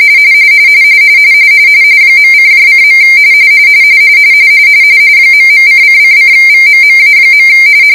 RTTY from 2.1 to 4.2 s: .::.